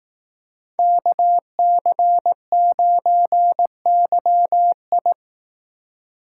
Morse code, heard as a keyed tone, about 18 wpm, 700 Hz. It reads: KC9YI
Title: Speed Racing - USA Call Signs 15wpm